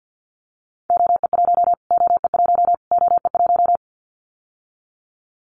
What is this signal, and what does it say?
Morse code, heard as a keyed tone, about 50 wpm, 700 Hz. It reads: OE1 OE1 OE1
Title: Callsign Prefixes for Top 25 DXCC - Repeated 3x - 50wpm